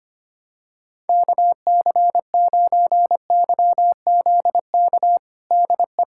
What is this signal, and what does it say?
Morse code, heard as a keyed tone, about 25 wpm, 700 Hz. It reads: KC9YZK DE
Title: USA Callsign DE Callsign 25wpm